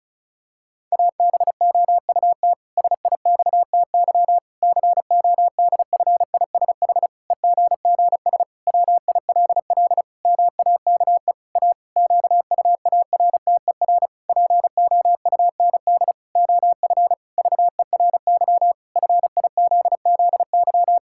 Morse code, heard as a keyed tone, about 36 wpm, 700 Hz. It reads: ABOUT SIXTY CODFISH EGGS WILL MAKE A QUARTER POUND OF VERY FIZZY
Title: Pangram Challenge v2 - 35wpm